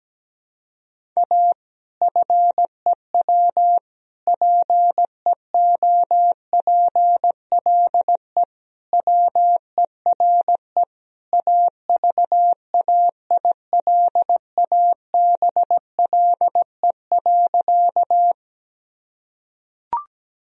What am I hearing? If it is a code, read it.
Morse code, heard as a keyed tone, about 17 wpm, 700 Hz. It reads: A FEW PEOPLE WERE AVAILABLE.
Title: Sentences from Top 1000 Words - Review Part 3 - 17wpm